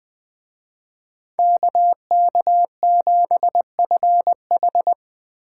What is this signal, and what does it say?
Morse code, heard as a keyed tone, about 20 wpm, 700 Hz. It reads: KK7FH